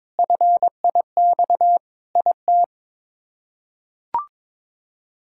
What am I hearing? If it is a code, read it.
Morse code, heard as a keyed tone, about 22 wpm, 700 Hz. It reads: FIX IT
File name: practice / 22wpm / Sets of 7 Words - 22wpm